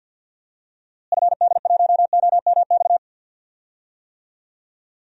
Morse code, heard as a keyed tone, about 52 wpm, 700 Hz. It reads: FD1OMX
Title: DXCC-100 Call Signs - 6 Characters - Repeated 2x - 50wpm